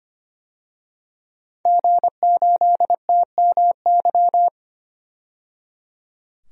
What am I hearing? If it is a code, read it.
Morse code, heard as a keyed tone, about 25 wpm, 690 Hz. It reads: G8TMY
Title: DXCC-100 Call Signs - 5 Characters - 25wpm